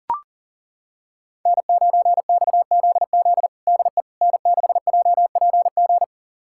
Morse code, heard as a keyed tone, about 40 wpm, 700 Hz. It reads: N9XZZ DE N6JPG
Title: USA Callsign DE Callsign - 5 Characters - 40wpm